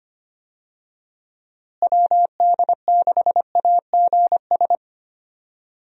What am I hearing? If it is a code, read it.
Morse code, heard as a keyed tone, about 25 wpm, 700 Hz. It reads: WD6AGS
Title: USA Call Signs - 6 Characters - Repeated 3x - 25wpm